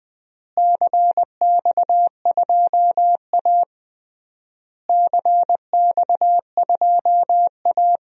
Morse code, heard as a keyed tone, about 21 wpm, 700 Hz. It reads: CX2A CX2A
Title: DXCC-100 Call Signs - 4 Characters - Repeated 3x - 20wpm